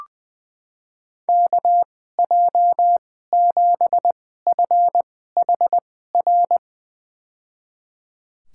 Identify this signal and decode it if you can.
Morse code, heard as a keyed tone, about 20 wpm, spaced as for 14 wpm, 700 Hz. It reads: KJ7FHR
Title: USA Call Signs - 6 Characters - 2x Character Spacing - 20wpm